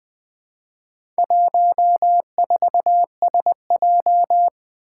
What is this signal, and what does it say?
Morse code, heard as a keyed tone, about 20 wpm, 700 Hz. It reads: 14SJ
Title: Four Challenging Letter-Number Combinations 20wpm